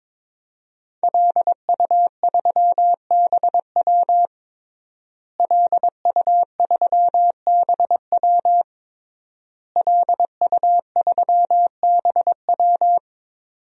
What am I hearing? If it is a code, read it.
Morse code, heard as a keyed tone, about 22 wpm, 700 Hz. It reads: LU3BW LU3BW LU3BW
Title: DXCC-100 Call Signs - 5 Characters - Repeated 3x - 22wpm